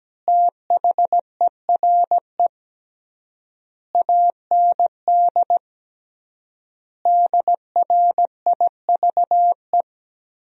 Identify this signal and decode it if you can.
Morse code, heard as a keyed tone, about 17 wpm, 700 Hz. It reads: THERE AND DRIVE
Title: 3x Word-Spacing - Sentences from Top 500 Words - 17wpm